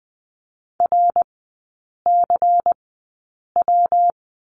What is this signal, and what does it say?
Morse code, heard as a keyed tone, about 19 wpm, spaced as for 8 wpm, 700 Hz. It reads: RCW
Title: 133 Lesson 22e - Question Mark - Three Characters 20wpm